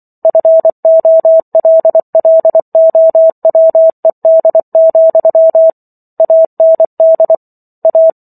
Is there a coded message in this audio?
Yes: Morse code from 0.2 to 8.1 s, heard as a keyed tone, about 24 wpm, 650 Hz. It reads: FOLLOWED, AND A